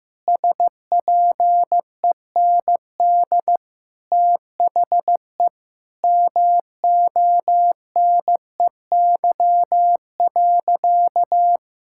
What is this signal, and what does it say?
Morse code, heard as a keyed tone, about 15 wpm, 700 Hz. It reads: SPEND THE MONEY.
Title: Sentences from Top 1000 Words - Review Part 5 - 15wpm